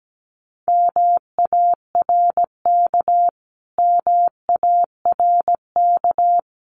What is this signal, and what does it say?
Morse code, heard as a keyed tone, about 17 wpm, 700 Hz. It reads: MARK MARK
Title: Top 500 Words Repeated 17wpm